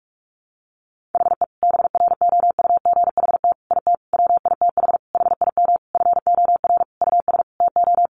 Morse code, heard as a keyed tone, about 44 wpm, 700 Hz. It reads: HE BROUGHT IT WITH HIM FOR US TO
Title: Sentences from Top 500 Words 45wpm